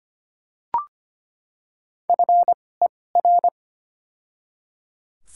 Morse code, heard as a keyed tone, about 25 wpm, spaced as for 18 wpm, 700 Hz. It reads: FER
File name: Top 20 QSO Elements - 2x Character Spacing - 25wpm